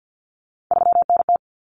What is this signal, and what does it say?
Morse code, heard as a keyed tone, about 50 wpm, 700 Hz. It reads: 3NT
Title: Common Cut Numbers - 3-Digit 50wpm